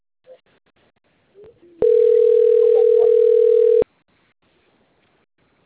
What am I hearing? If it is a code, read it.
Ringback tone (North America), 440 + 480 Hz.